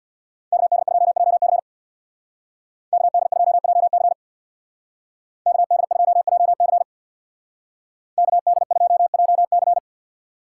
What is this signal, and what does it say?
Morse code, heard as a keyed tone, about 63 wpm, 700 Hz. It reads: KD2JC KD2JC KD2JC KD2JC